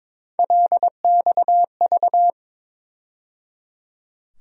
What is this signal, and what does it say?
Morse code, heard as a keyed tone, about 22 wpm, 700 Hz. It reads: LXV
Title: USA Call Sign Postfixes - 3 Letters 22wpm